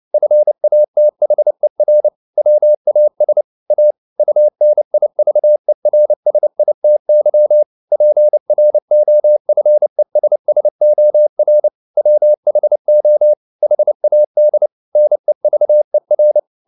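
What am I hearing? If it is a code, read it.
Morse code, heard as a keyed tone, about 29 wpm, 600 Hz. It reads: FATHER WAS A UNIVERSITY PROFESSOR WHO HAD NEVER
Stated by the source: Morse code at 29 wpm, sending Ken Follett's novel Never in English, from CW med Gnister.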